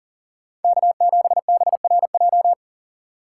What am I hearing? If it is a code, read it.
Morse code, heard as a keyed tone, about 40 wpm, 700 Hz. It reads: K7BRJ
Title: USA Call Signs - 5 Characters - 2x Character Spacing - 40wpm